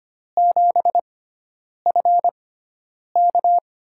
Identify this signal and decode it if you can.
Morse code, heard as a keyed tone, about 25 wpm, spaced as for 9 wpm, 700 Hz. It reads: 7FK
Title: USA Call Signs - 6x Character Spacing - 25wpm